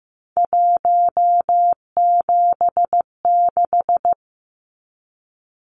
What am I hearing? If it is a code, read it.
Morse code, heard as a keyed tone, about 15 wpm, 700 Hz. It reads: 176
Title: Numbers - 3-Digit 15wpm